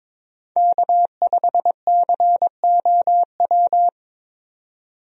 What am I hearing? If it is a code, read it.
Morse code, heard as a keyed tone, about 22 wpm, 700 Hz. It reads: K5COW
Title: USA Call Signs - 5 Characters - 22wpm